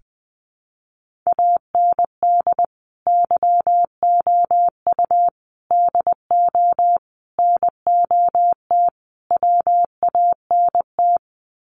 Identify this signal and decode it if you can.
Morse code, heard as a keyed tone, about 20 wpm, 700 Hz. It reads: AND YOU DO NOT WANT
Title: Sets of 5 Words Encore2 20wpm